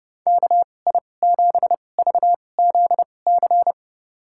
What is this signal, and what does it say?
Morse code, heard as a keyed tone, about 30 wpm, spaced as for 22 wpm, 700 Hz. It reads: KI7VZC
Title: USA Call Signs - 6 Characters - 2x Character Spacing - 30wpm